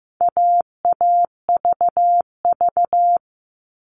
Morse code, heard as a keyed tone, about 15 wpm, 700 Hz. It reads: AAVV
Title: All Cut Numbers - 4-Digit 15wpm